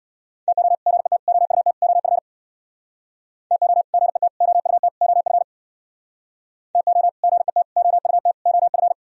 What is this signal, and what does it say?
Morse code, heard as a keyed tone, about 64 wpm, 700 Hz. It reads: TO GET OUT OF TO GET OUT OF TO GET OUT OF